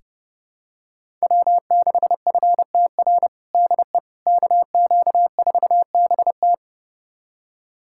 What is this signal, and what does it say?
Morse code, heard as a keyed tone, about 30 wpm, 700 Hz. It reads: W6FTR DE KQ4BT
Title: USA Callsign DE Callsign - 5 Characters - 30wpm